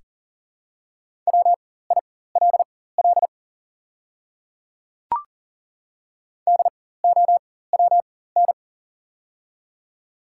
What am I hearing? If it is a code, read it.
Morse code, heard as a keyed tone, about 40 wpm, spaced as for 19 wpm, 700 Hz. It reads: WILL DOWN
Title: 4x Character Spacing - Top 300 Words - 40wpm